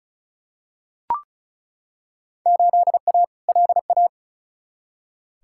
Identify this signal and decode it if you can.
Morse code, heard as a keyed tone, about 36 wpm, 700 Hz. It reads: 8A LA
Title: Field Day Exchanges Encore - 35wpm